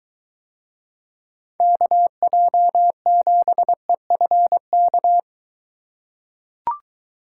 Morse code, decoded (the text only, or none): KJ7EFK